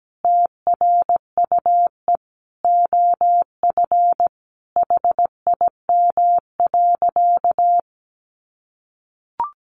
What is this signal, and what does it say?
Morse code, heard as a keyed tone, about 17 wpm, 700 Hz. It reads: TRUE OF HIM.
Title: Sentences from Top 700 Words 17wpm